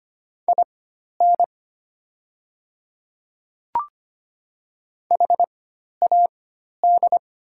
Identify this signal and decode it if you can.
Morse code, heard as a keyed tone, about 25 wpm, spaced as for 12 wpm, 690 Hz. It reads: IN HAD
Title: 4x Character Spacing - Top 300 Words - 25wpm